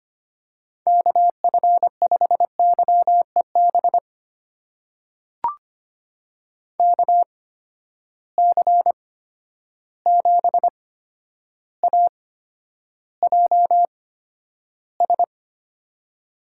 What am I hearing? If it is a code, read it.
Morse code, heard as a keyed tone, about 25 wpm, spaced as for 11 wpm, 700 Hz. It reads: KF5YEB KC7AJS
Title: USA Call Signs - 6 Characters - 8x Character Spacing - 25wpm